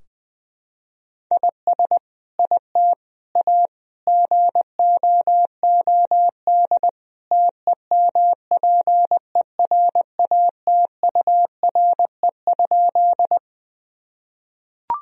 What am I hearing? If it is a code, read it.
Morse code, heard as a keyed tone, about 20 wpm, 700 Hz. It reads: IS IT A GOOD TEMPERATURE?
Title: Sentences from Top 1000 Words - Review Part 4 - 20wpm